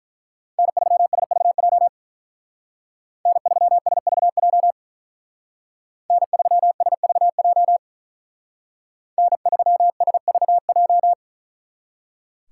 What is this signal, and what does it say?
Morse code, heard as a keyed tone, about 43 wpm, 700 Hz. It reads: N3SVJ N3SVJ N3SVJ N3SVJ
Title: Speed Racing - USA 1x3 Call Signs 35wpm